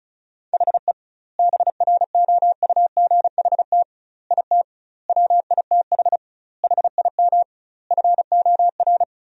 Morse code, heard as a keyed tone, about 34 wpm, 700 Hz. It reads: HE BROUGHT IT WITH HIM FOR
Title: 2x Word-Spacing - Sentences from Top 500 Words - 35wpm